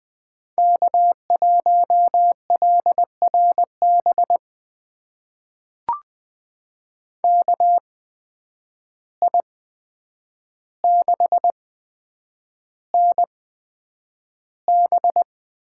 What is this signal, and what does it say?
Morse code, heard as a keyed tone, about 20 wpm, spaced as for 9 wpm, 700 Hz. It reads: K1LRB KI6NB
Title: USA Call Signs - 5 Characters - 8x Character Spacing - 20wpm